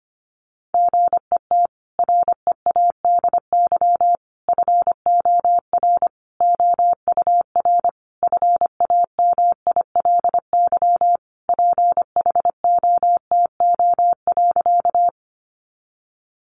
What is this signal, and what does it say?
Morse code, heard as a keyed tone, about 25 wpm, 700 Hz. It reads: GET READY FOR OUR FAMILY PHOTO.
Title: Sentences from Top 1200 Words 25wpm